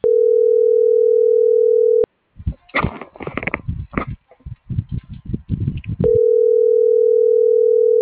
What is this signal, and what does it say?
Ringback tone (North America), 440 + 480 Hz.